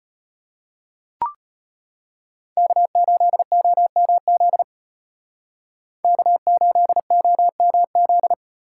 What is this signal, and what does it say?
Morse code, heard as a keyed tone, about 36 wpm, 700 Hz. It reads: K8OMZ K8OMZ